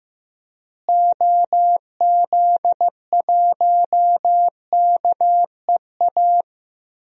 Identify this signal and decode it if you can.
Morse code, heard as a keyed tone, about 15 wpm, 700 Hz. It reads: OZ1KEA
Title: DXCC-100 Call Signs - 6 Characters - Repeated 2x - 15wpm